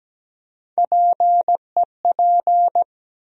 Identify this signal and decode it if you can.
Morse code, heard as a keyed tone, about 17 wpm, 700 Hz. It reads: PEP
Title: USA Call Sign Postfixes - 3 Letters 17wpm